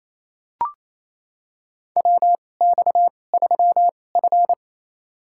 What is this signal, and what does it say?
Morse code, heard as a keyed tone, about 28 wpm, spaced as for 20 wpm, 700 Hz. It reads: WX3F
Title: USA Call Signs - 4 Characters - 2x Character Spacing - 28wpm